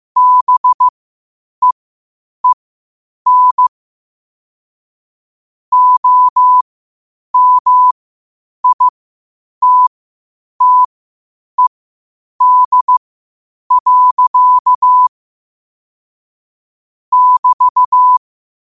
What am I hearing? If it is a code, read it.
Morse code, heard as a keyed tone, about 15 wpm, spaced as for 8 wpm, 1000 Hz. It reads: BEEN OMITTED. <BT>
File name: qotdfiles / 8wpm